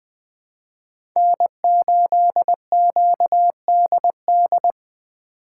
Morse code, heard as a keyed tone, about 20 wpm, 700 Hz. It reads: N8QDD
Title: USA Call Signs - 2x Character Spacing - 20wpm